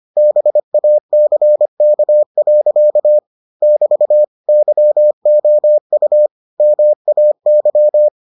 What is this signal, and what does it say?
Morse code, heard as a keyed tone, about 25 wpm, 600 Hz. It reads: BACK. <BT> YOU MAY